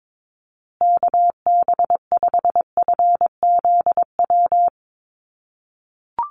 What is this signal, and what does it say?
Morse code, heard as a keyed tone, about 22 wpm, 700 Hz. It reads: KB5FZW
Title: USA Call Signs Repeated 3x 22wpm